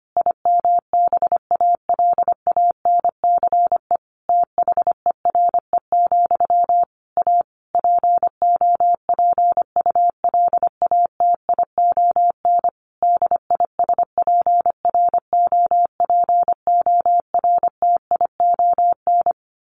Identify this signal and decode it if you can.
Morse code, heard as a keyed tone, about 25 wpm, 700 Hz. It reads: IMBALANCE THERE, A POPULATION DISPROPORTION
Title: A Sound Of Thunder 25wpm